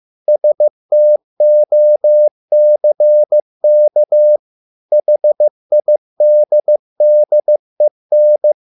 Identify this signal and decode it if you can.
Morse code, heard as a keyed tone, about 15 wpm, 600 Hz. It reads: STOCK HIDDEN